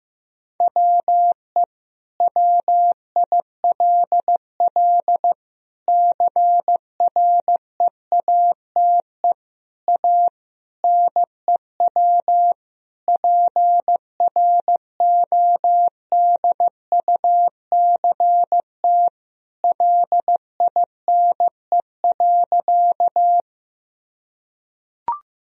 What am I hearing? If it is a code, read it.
Morse code, heard as a keyed tone, about 15 wpm, 700 Hz. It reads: WE WILL CREATE A NEW PRODUCT LINE.
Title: Sentences from Top 1000 Words - Review Part 3 - 15wpm